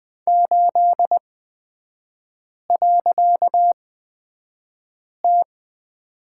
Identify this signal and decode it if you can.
Morse code, heard as a keyed tone, about 20 wpm, spaced as for 5 wpm, 700 Hz. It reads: 8.T
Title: 255 Lesson 41e - Period - Three Characters 20wpm